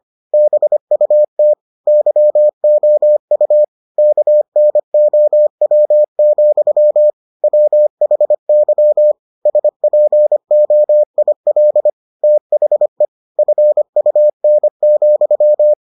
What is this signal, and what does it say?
Morse code, heard as a keyed tone, about 25 wpm, 600 Hz. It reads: BUT YOU KNOW, WHY SPOIL THE FUN,